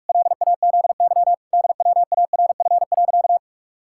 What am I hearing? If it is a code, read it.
Morse code, heard as a keyed tone, about 41 wpm, 700 Hz. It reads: LAZY DWARF.
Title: Pangram Challenge v2 - 45wpm